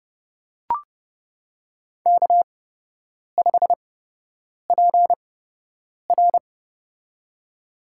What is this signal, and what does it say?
Morse code, heard as a keyed tone, about 31 wpm, spaced as for 8 wpm, 700 Hz. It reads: K5PR
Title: USA Call Signs - 8x Character Spacing - 30wpm